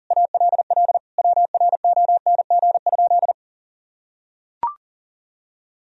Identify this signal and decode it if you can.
Morse code, heard as a keyed tone, about 37 wpm, 700 Hz. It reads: ALL WRONG?
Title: Sentences from Top 1000 Words - Review Part 1 - 40wpm